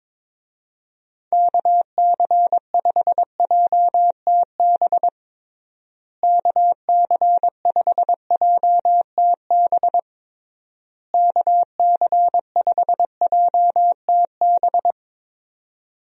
Morse code, heard as a keyed tone, about 22 wpm, 700 Hz. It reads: KC5JTB KC5JTB KC5JTB
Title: USA Call Signs - 6 Characters - Repeated 3x - 22wpm